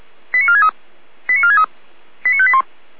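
Five-tone selective call (ZVEI), heard as quick runs of five tones, tones of about 80 ms. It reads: 89463 89463 89581
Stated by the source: ZVEI five-tone call 89361 received by SDR, archived by fire brigade Feuerwehr Orlamünde.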